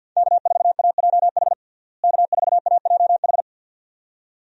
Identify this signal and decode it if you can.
Morse code, heard as a keyed tone, about 52 wpm, 700 Hz. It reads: K4AJH K4AJH
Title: USA Call Signs - 5 Characters - Repeated 2x - 50wpm